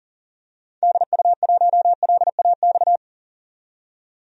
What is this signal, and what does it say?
Morse code, heard as a keyed tone, about 40 wpm, 700 Hz. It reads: DU1LAX
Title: DXCC-100 Call Signs - 6 Characters - 40wpm